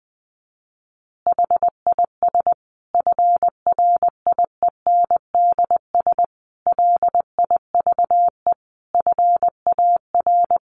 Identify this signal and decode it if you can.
Morse code, heard as a keyed tone, about 20 wpm, 700 Hz. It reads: HIS FRIENDS LIVE FAR
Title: Additional Easy Phrases 20wpm